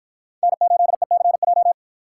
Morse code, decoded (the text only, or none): N8ECJ